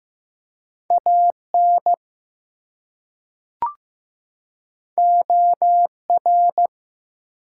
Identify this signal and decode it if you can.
Morse code, heard as a keyed tone, about 15 wpm, 700 Hz. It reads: AN OR
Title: Two Letter Words - 15wpm